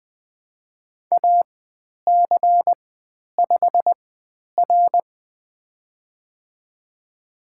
Morse code, heard as a keyed tone, about 20 wpm, spaced as for 10 wpm, 700 Hz. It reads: AC5R
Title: 095 Lesson 16g - Letter c - Call Signs 20wpm